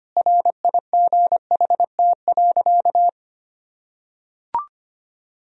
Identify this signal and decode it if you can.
Morse code, heard as a keyed tone, about 25 wpm, 700 Hz. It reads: RIGHT.
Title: Sentences from Top 1000 Words - Review Part 4 - 25wpm